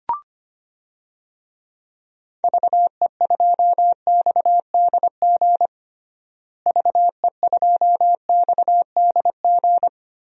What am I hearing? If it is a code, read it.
Morse code, heard as a keyed tone, about 25 wpm, 700 Hz. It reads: VE2XDG VE2XDG
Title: DXCC-100 Call Signs - 6 Characters - Repeated 3x - 25wpm